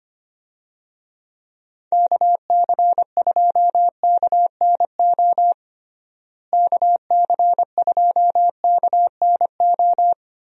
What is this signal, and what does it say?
Morse code, heard as a keyed tone, about 25 wpm, 700 Hz. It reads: KC2KNO KC2KNO
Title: USA Call Signs - 6 Characters - Repeated 2x - 25wpm